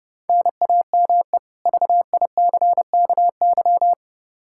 Morse code, heard as a keyed tone, about 30 wpm, 700 Hz. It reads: NAME VICKY
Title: Practice QSO - Exchange 1 - Beginning - 30wpm